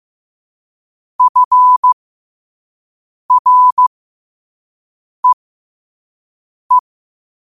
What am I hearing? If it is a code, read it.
Morse code, heard as a keyed tone, about 15 wpm, spaced as for 5 wpm, 1000 Hz. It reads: FREE
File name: qotdfiles / 5wpm